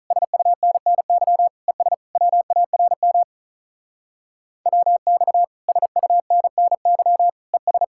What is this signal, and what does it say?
Morse code, heard as a keyed tone, about 46 wpm, 700 Hz. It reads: SUNNY ES WARM WX SUNNY ES